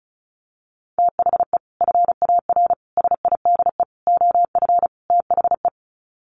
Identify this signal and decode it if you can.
Morse code, heard as a keyed tone, about 34 wpm, 700 Hz. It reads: THE FAR SIDE OF THE
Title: Sets of 5 Words 35wpm